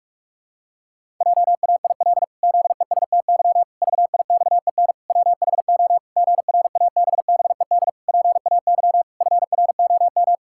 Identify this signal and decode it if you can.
Morse code, heard as a keyed tone, about 41 wpm, 700 Hz. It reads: JAIL ZESTY VIXEN WHO GRABBED PAY FROM
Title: Pangram Challenge v2 - 45wpm